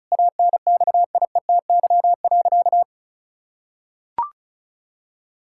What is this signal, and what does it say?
Morse code, heard as a keyed tone, about 34 wpm, 700 Hz. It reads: ANXIETY.
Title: Sentences from Top 1600 Words 35wpm